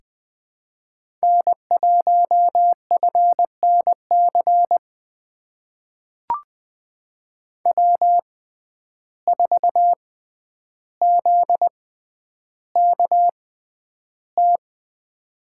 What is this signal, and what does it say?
Morse code, heard as a keyed tone, about 20 wpm, spaced as for 10 wpm, 700 Hz. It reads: N1FNC W4ZKT
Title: USA Call Signs - 5 Characters - 6x Character Spacing - 20wpm